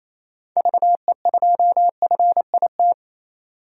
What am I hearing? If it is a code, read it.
Morse code, heard as a keyed tone, about 28 wpm, 700 Hz. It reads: VE2FIT
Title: DXCC-100 Call Signs - 6 Characters - 28wpm